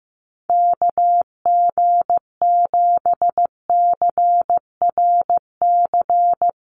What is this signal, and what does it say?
Morse code, heard as a keyed tone, about 15 wpm, 700 Hz. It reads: KG7CRC
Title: USA Call Signs 15wpm